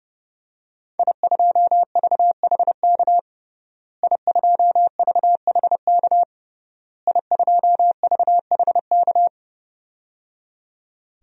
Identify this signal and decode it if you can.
Morse code, heard as a keyed tone, about 30 wpm, 700 Hz. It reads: I2VHK I2VHK I2VHK
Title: DXCC-100 Call Signs - 5 Characters - Repeated 3x - 30wpm